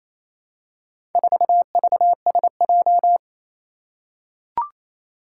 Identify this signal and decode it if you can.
Morse code, heard as a keyed tone, about 27 wpm, 700 Hz. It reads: 4VSJ
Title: Four Challenging Letter-Number Combinations 28wpm